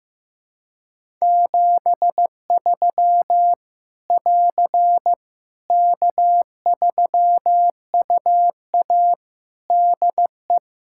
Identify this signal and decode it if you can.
Morse code, heard as a keyed tone, about 15 wpm, 700 Hz. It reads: 73 <AR> K3UA DE
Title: Practice QSO - Exchange 4 - Ending - 15wpm